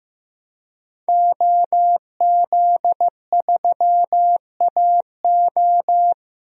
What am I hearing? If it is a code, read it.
Morse code, heard as a keyed tone, about 15 wpm, 700 Hz. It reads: OZ3AO
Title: DXCC-100 Call Signs - 5 Characters - Repeated 2x - 15wpm